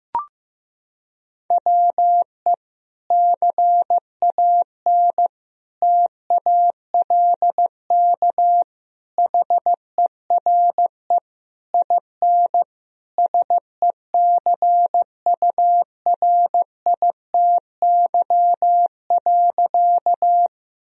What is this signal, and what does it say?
Morse code, heard as a keyed tone, about 15 wpm, 700 Hz. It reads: WE CAN TALK HERE IN SECURITY.
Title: Sentences from Top 1000 Words - Review Part 4 - 15wpm